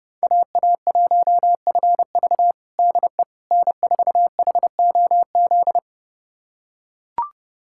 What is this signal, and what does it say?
Morse code, heard as a keyed tone, about 30 wpm, 700 Hz. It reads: AA1FV DE N4HOZ
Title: USA Callsign DE Callsign - 5 Characters - 30wpm